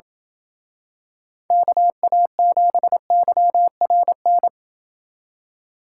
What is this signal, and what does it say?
Morse code, heard as a keyed tone, about 28 wpm, 700 Hz. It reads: KA7YRN